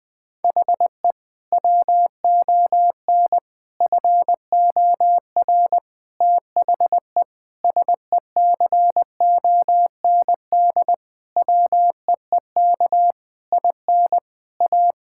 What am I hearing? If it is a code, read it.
Morse code, heard as a keyed tone, about 20 wpm, 700 Hz. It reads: HE WON FOR THE SECOND WEEK IN A
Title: Sentences with 7-Word Phrases - 20wpm